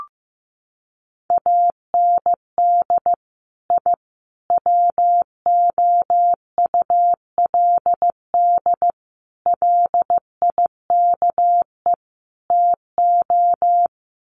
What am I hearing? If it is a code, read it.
Morse code, heard as a keyed tone, about 15 wpm, 700 Hz. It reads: AND I WOULD LIKE TO
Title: Sets of 5 Words Encore 15wpm